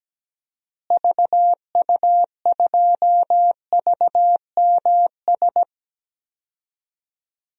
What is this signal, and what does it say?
Morse code, heard as a keyed tone, about 17 wpm, 700 Hz. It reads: VU2VMS
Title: DXCC-100 Call Signs - 6 Characters - Repeated 2x - 17wpm